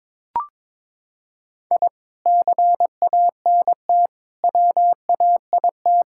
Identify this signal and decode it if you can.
Morse code, heard as a keyed tone, about 22 wpm, 700 Hz. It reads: I CANT WAIT
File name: practice / 22wpm / Sentences with 3-Word Phrases - 22wpm